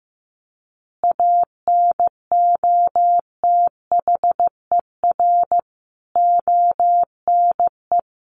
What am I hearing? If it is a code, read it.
Morse code, heard as a keyed tone, about 14 wpm, 700 Hz. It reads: ANOTHER ONE
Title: Additional Easy Phrases 15wpm